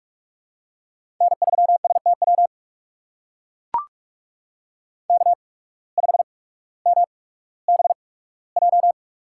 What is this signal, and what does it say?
Morse code, heard as a keyed tone, about 44 wpm, spaced as for 19 wpm, 700 Hz. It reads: N3STW K5MBJ
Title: USA Call Signs - 5 Characters - 8x Character Spacing - 45wpm